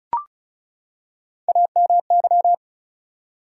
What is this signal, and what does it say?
Morse code, heard as a keyed tone, about 35 wpm, 700 Hz. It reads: AMY
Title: USA Common Names - Max 4 Letters - 35wpm